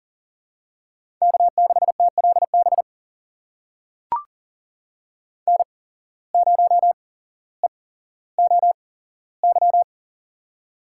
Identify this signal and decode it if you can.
Morse code, heard as a keyed tone, about 40 wpm, spaced as for 17 wpm, 700 Hz. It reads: K6TLB N0EOY
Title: USA Call Signs - 5 Characters - 8x Character Spacing - 40wpm